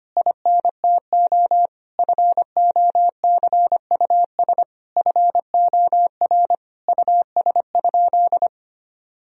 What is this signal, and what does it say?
Morse code, heard as a keyed tone, about 25 wpm, 700 Hz. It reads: INTO FOCUS FOR US?
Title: Sentences from Top 1000 Words - Review Part 4 - 25wpm